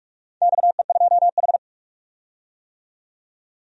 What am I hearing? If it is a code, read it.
Morse code, heard as a keyed tone, about 44 wpm, 700 Hz. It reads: XE2H
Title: DXCC-100 Call Signs - 4 Characters - 45wpm